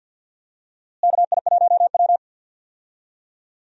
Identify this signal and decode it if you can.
Morse code, heard as a keyed tone, about 51 wpm, 700 Hz. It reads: KI1W
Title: USA Call Signs - 4 Characters - Repeated 3x - 50wpm